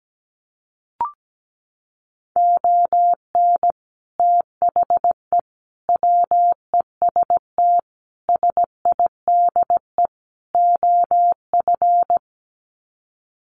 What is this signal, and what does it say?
Morse code, heard as a keyed tone, about 17 wpm, 700 Hz. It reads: ON THE WEST SIDE OF
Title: Sets of 5 Words Encore2 17wpm